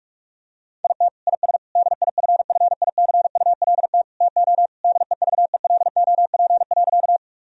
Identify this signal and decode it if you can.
Morse code, heard as a keyed tone, about 44 wpm, 700 Hz. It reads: IT IS DIFFICULT TO DEVELOP.
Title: Sentences from Top 1000 Words - Review Part 2 - 45wpm